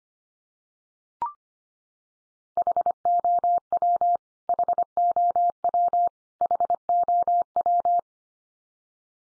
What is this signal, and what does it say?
Morse code, heard as a keyed tone, about 25 wpm, 700 Hz. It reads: HOW HOW HOW
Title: Top 100 Words 25wpm x3